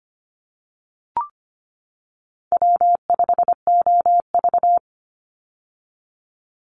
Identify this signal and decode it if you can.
Morse code, heard as a keyed tone, about 25 wpm, 700 Hz. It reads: W5OV
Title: Easy Call Signs - 25wpm